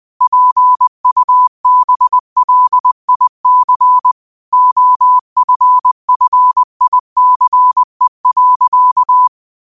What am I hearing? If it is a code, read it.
Morse code, heard as a keyed tone, about 20 wpm, 1000 Hz. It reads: PUBLIC OFFICE.
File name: 20wpm